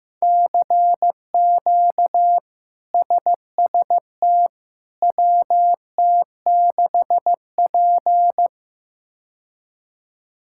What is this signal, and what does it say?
Morse code, heard as a keyed tone, about 15 wpm, 700 Hz. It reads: CQ SST WT6P
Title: K1USN SST QSO Practice - 15wpm